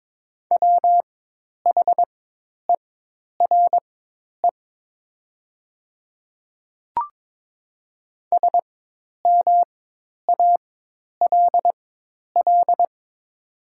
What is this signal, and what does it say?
Morse code, heard as a keyed tone, about 23 wpm, spaced as for 10 wpm, 700 Hz. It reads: WHERE SMALL
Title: 4x Character Spacing - Top 300 Words - 22wpm